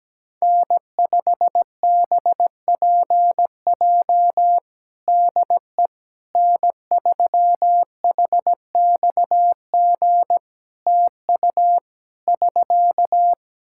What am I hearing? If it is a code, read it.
Morse code, heard as a keyed tone, about 17 wpm, 700 Hz. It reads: N5BPJ DE N3HXG TU <SK>
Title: K1USN SST QSO Practice 17wpm